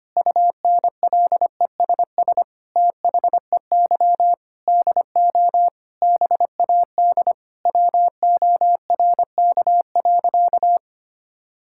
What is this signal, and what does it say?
Morse code, heard as a keyed tone, about 25 wpm, 700 Hz. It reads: UNLESS THEY DO BAD WORK.
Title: Sentences from Top 1000 Words - Review Part 6 - 25wpm